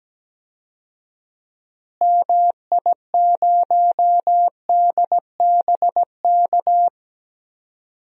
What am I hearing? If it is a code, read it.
Morse code, heard as a keyed tone, about 17 wpm, 700 Hz. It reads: MI0DBK
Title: DXCC-100 Call Signs - 6 Characters - Repeated 2x - 17wpm